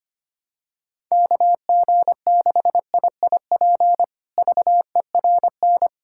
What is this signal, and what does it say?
Morse code, heard as a keyed tone, about 25 wpm, 700 Hz. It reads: KG6IIP VERN
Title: K1USN SST QSO Practice - 25wpm